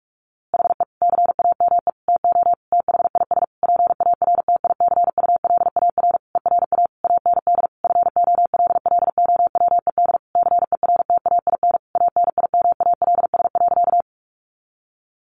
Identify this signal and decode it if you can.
Morse code, heard as a keyed tone, about 44 wpm, 700 Hz. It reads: HE CAME TO THIS PARTICULAR ERA AND FOLLOWED CERTAIN ANIMALS.
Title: A Sound Of Thunder 45wpm